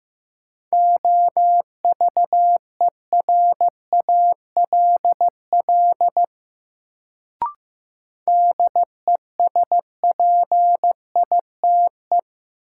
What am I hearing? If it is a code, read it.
Morse code, heard as a keyed tone, about 15 wpm, 700 Hz. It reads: OVERALL DESPITE
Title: Sentences from Top 1000 Words - Review Part 7 - New Word Drill - 15wpm